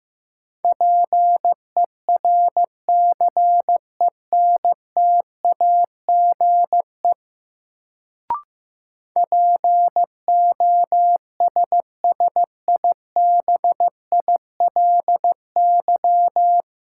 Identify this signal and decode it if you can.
Morse code, heard as a keyed tone, about 15 wpm, 700 Hz. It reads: PERCENTAGE POSSIBILY
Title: Sentences from Top 1300 Words - New Word Drill Alt - 15wpm